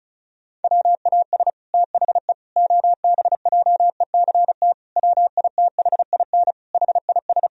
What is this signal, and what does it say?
Morse code, heard as a keyed tone, about 36 wpm, 700 Hz. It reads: WAS THE OBJECT WITHIN HIS
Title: Sentences from Top 1000 Words - Review Part 4 - 35wpm